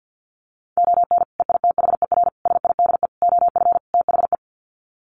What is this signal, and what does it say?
Morse code, heard as a keyed tone, about 48 wpm, 700 Hz. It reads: ON EITHER SIDE OF THE
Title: Sets of 5 Words 50wpm